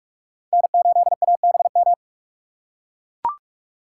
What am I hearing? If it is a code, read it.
Morse code, heard as a keyed tone, about 41 wpm, 700 Hz. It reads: N8ABM
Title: USA Call Signs - 5 Characters - 3x Character Spacing - 45wpm